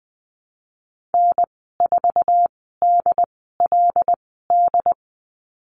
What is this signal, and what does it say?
Morse code, heard as a keyed tone, about 20 wpm, spaced as for 15 wpm, 700 Hz. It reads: N4DLD
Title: 161 Lesson 26g - Letter g - Call Signs 20wpm